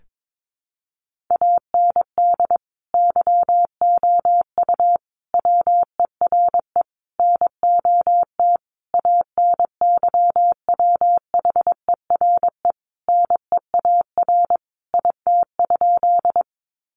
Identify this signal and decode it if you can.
Morse code, heard as a keyed tone, about 22 wpm, 700 Hz. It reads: AND YOU WERE NOT ANYWHERE NEAR IT?
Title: Sentences from Top 1400 Words 22wpm